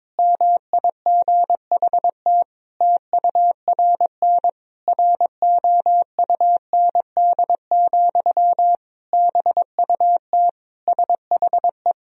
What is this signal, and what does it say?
Morse code, heard as a keyed tone, about 22 wpm, 700 Hz. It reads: MIGHT TURN ROUND, BUT SHE
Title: Sentences from Top 1000 Words - Review Part 1 - 22wpm